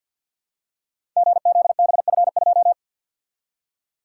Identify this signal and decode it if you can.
Morse code, heard as a keyed tone, about 51 wpm, 700 Hz. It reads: GZBF2
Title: Code Groups - 5 Letter:Numbers - 50wpm